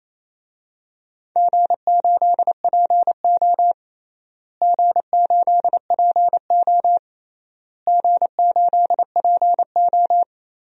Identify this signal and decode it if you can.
Morse code, heard as a keyed tone, about 28 wpm, 700 Hz. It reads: G8PO G8PO G8PO
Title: DXCC-100 Call Signs - 4 Characters - Repeated 3x - 28wpm